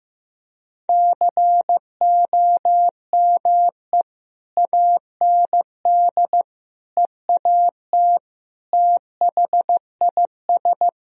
Morse code, heard as a keyed tone, about 15 wpm, 700 Hz. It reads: COME AND EAT THIS
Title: Sentences from Top 1000 Words - Review Part 1 - 15wpm